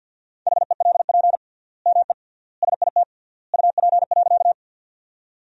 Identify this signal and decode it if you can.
Morse code, heard as a keyed tone, about 52 wpm, 700 Hz. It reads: HELP ME SIT UP.
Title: 3x Word-Spacing - Sentences from Top 500 Words - 50wpm